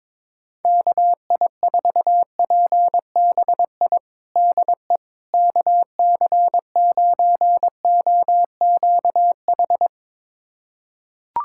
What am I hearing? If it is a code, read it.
Morse code, heard as a keyed tone, about 22 wpm, 700 Hz. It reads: KI4PBI DE KC9OQH
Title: USA Callsign DE Callsign 22wpm